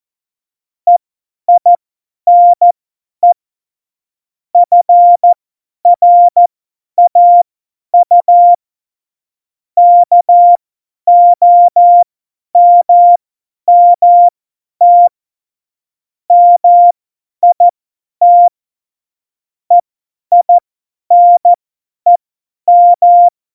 Morse code, heard as a keyed tone, about 14 wpm, spaced as for 10 wpm, 700 Hz. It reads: EINE FRAU KOMMT MIT EINEM
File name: Morsekurs / Witze / Witz 27 March 2018, 50 F70